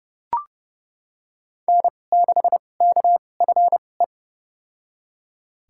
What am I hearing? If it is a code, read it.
Morse code, heard as a keyed tone, about 30 wpm, spaced as for 22 wpm, 700 Hz. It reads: N6KFE
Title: USA Call Signs - 5 Characters - 2x Character Spacing - 30wpm